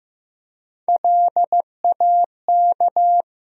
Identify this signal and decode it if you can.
Morse code, heard as a keyed tone, about 15 wpm, 700 Hz. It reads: LAK
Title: USA Call Sign Postfixes - 3 Letters 15wpm